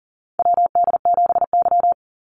Morse code, heard as a keyed tone, about 40 wpm, 700 Hz. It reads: WD7Y
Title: NA SOTA Chaser Callsigns 40wpm